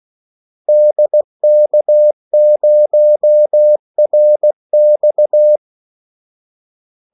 Morse code, heard as a keyed tone, about 16 wpm, 600 Hz. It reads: DK0RX
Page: Kaldesignaler 16 wpm